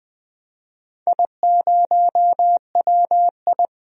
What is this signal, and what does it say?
Morse code, heard as a keyed tone, about 20 wpm, 700 Hz. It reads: I0WI
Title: DXCC-100 Call Signs - 4 Characters - 20wpm